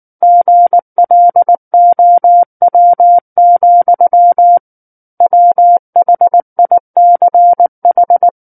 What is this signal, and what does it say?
Morse code, heard as a keyed tone, about 19 wpm, 700 Hz. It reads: GLOW, WHICH
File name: Book One / 10-Chapter 10 - 19 WPM